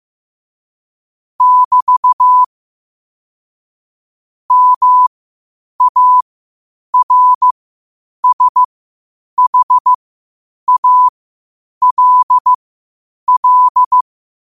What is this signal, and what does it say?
Morse code, heard as a keyed tone, about 15 wpm, spaced as for 8 wpm, 1000 Hz. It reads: <BT> MARSHALL
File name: qotdfiles / 8wpm